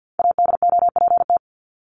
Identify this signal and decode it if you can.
Morse code, heard as a keyed tone, about 49 wpm, 700 Hz. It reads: ADOPT
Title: ED Suffix - Top 100 Words with Base Words 50wpm